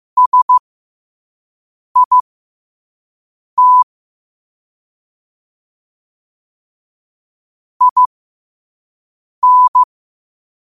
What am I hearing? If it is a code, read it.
Morse code, heard as a keyed tone, about 15 wpm, spaced as for 5 wpm, 1000 Hz. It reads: SIT IN